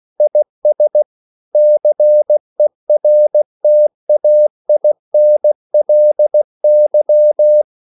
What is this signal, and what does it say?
Morse code, heard as a keyed tone, about 16 wpm, 600 Hz. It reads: IS CERTAINLY